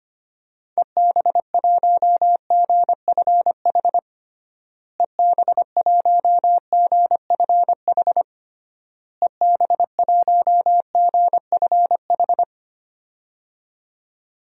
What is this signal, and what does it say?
Morse code, heard as a keyed tone, about 25 wpm, 700 Hz. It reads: EB1GFH EB1GFH EB1GFH
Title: DXCC-100 Call Signs - 6 Characters - Repeated 3x - 25wpm